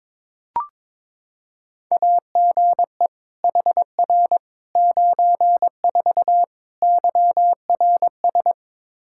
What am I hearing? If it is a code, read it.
Morse code, heard as a keyed tone, about 22 wpm, 700 Hz. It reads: AGE HR 94 YRS
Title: Practice QSO - Exchange 3 - More Info - 22wpm